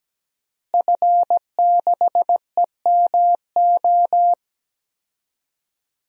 Morse code, heard as a keyed tone, about 17 wpm, 700 Hz. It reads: F6EMO
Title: DXCC-100 Call Signs - 5 Characters - Repeated 3x - 17wpm